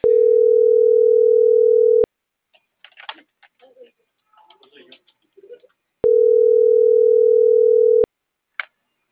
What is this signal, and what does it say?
Ringback tone (North America), 440 + 480 Hz.